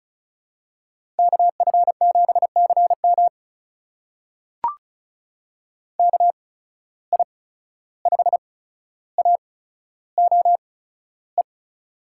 Morse code, heard as a keyed tone, about 36 wpm, spaced as for 14 wpm, 700 Hz. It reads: KF7CM KI5AOE
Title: USA Call Signs - 8x Character Spacing - 35wpm